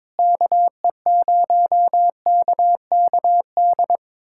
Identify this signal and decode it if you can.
Morse code, heard as a keyed tone, about 22 wpm, 700 Hz. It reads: KE0KKD
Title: USA Callsign DE Callsign - 6 Characters - 22wpm